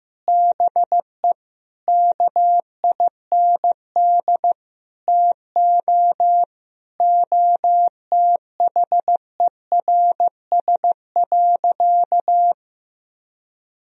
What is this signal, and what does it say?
Morse code, heard as a keyed tone, about 15 wpm, 700 Hz. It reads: BE KIND TO OTHERS.
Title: Sentences from Top 1000 Words - Review Part 4 - 15wpm